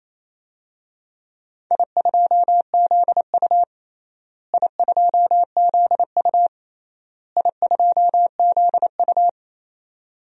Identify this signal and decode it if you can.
Morse code, heard as a keyed tone, about 28 wpm, 700 Hz. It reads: I2ZU I2ZU I2ZU
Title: DXCC-100 Call Signs - 4 Characters - Repeated 3x - 28wpm